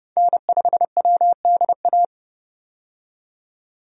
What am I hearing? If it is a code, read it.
Morse code, heard as a keyed tone, about 30 wpm, 700 Hz. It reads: N5WDA
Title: USA Call Signs - 5 Characters - 30wpm